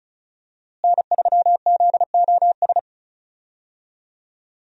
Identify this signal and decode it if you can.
Morse code, heard as a keyed tone, about 34 wpm, 700 Hz. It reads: N3ZOS
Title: USA Call Signs - 5 Characters - Repeated 2x - 35wpm